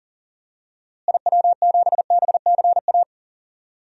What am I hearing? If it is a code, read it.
Morse code, heard as a keyed tone, about 40 wpm, 700 Hz. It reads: IW7BCA
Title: DXCC-100 Call Signs - 6 Characters - Repeated 2x - 40wpm